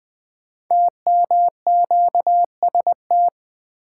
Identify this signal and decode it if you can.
Morse code, heard as a keyed tone, about 20 wpm, 700 Hz. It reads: TMQST
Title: Code Groups - 5 Letter:Numbers - 20wpm